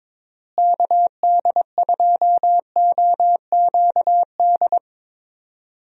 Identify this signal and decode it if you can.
Morse code, heard as a keyed tone, about 21 wpm, 700 Hz. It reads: KD2OQD
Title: USA Call Signs - 6 Characters - Repeated 3x - 22wpm